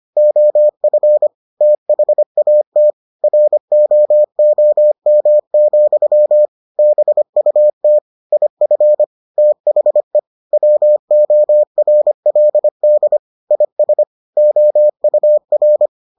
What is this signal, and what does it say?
Morse code, heard as a keyed tone, about 25 wpm, 600 Hz. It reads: OF THAT ROOM, BUT IF THE WORLD IS OUR